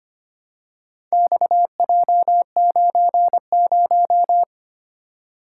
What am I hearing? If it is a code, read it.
Morse code, heard as a keyed tone, about 25 wpm, 700 Hz. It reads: XJ90
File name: practice / 25wpm / Four Letter-Number Combinations - 25wpm